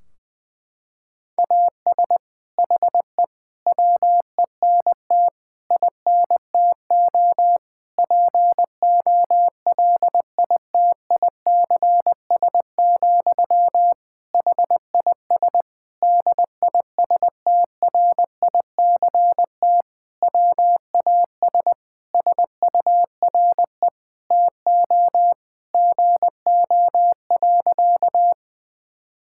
Morse code, heard as a keyed tone, about 21 wpm, 700 Hz. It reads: AS HE WENT INTO POLITICS, HIS DISTRICT WAS SURE TO GO.
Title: Sentences from Top 1600 Words 20wpm